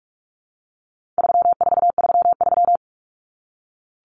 Morse code, heard as a keyed tone, about 44 wpm, 700 Hz. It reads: 3433
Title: Numbers - 4-Digit 45wpm